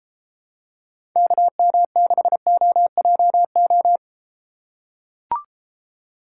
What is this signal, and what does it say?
Morse code, heard as a keyed tone, about 34 wpm, 700 Hz. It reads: KM6OJO